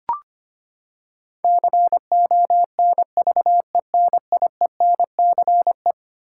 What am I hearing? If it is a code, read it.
Morse code, heard as a keyed tone, about 25 wpm, 700 Hz. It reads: CONVENIENCE
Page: morse-code-ninja-practice-25wpm